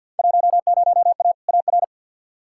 Morse code, heard as a keyed tone, about 51 wpm, 700 Hz. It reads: 10A AR
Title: Field Day Exchanges Encore - 50wpm